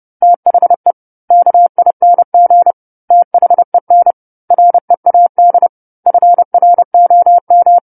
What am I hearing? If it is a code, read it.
Morse code, heard as a keyed tone, about 30 wpm, 700 Hz. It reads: THE KING THEN READ FROM
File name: 10-Chapter-10 30 WPM